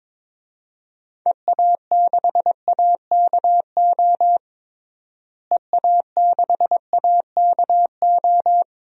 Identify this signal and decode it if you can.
Morse code, heard as a keyed tone, about 22 wpm, 700 Hz. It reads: EA6AKO EA6AKO
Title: DXCC-100 Call Signs - 6 Characters - Repeated 3x - 22wpm